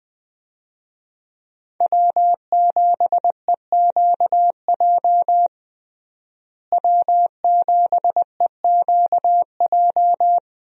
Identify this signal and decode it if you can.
Morse code, heard as a keyed tone, about 20 wpm, 700 Hz. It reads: W7EQJ W7EQJ
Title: USA Call Signs - 5 Characters - Repeated 2x - 20wpm